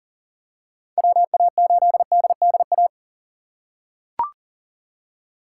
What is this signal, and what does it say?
Morse code, heard as a keyed tone, about 40 wpm, 700 Hz. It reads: WA8DDA
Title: USA Call Signs - 6 Characters - 4x Character Spacing - 40wpm